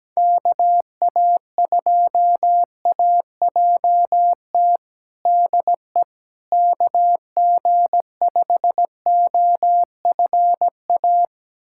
Morse code, heard as a keyed tone, about 17 wpm, 700 Hz. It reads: KA2AJT DE KG5OFA
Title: USA Callsign DE Callsign - 6 Characters - 17wpm